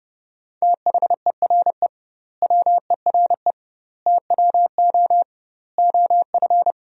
Morse code, heard as a keyed tone, about 30 wpm, 700 Hz. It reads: THERE WERE TWO OF
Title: 2x Word-Spacing - Sentences from Top 100 Words - 30wpm